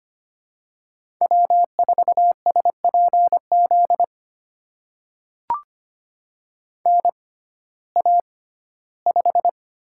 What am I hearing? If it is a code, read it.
Morse code, heard as a keyed tone, about 25 wpm, spaced as for 15 wpm, 700 Hz. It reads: W4SPZ NA5
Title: USA Call Signs - 6x Character Spacing - 25wpm